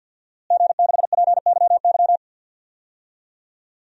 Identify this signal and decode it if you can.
Morse code, heard as a keyed tone, about 46 wpm, 700 Hz. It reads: G6PYY
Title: DXCC-100 Call Signs - 5 Characters - Repeated 3x - 50wpm